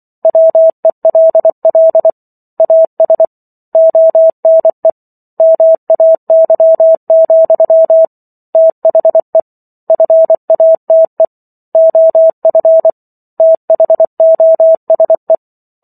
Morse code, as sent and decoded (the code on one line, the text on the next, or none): .-- . .-.. .-.. / .- ... / --- -. / -- .- -.-- --..-- / - .... / ..-. .- - . / --- ..-. / - .... --- ... .
WELL AS ONE MAY, THE FATE OF THOSE